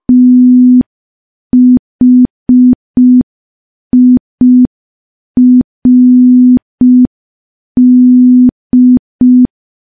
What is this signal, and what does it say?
Morse code, heard as a keyed tone, about 5 wpm, 250 Hz. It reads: THIRD
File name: morse / morse-training-05WPM